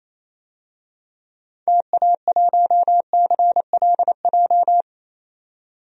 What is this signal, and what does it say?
Morse code, heard as a keyed tone, about 27 wpm, 700 Hz. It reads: TA1CLJ
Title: DXCC-100 Call Signs - 6 Characters - 28wpm